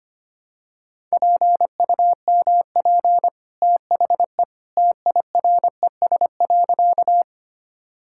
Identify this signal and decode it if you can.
Morse code, heard as a keyed tone, about 25 wpm, 700 Hz. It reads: PUMP THE TIRES.
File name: Easy Sentences with Hints - 25wpm